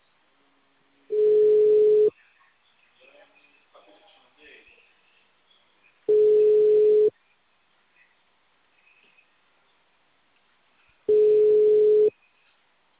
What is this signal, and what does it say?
Ringback tone (Europe, 425 Hz), with speech.
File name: out- -2004- - - .3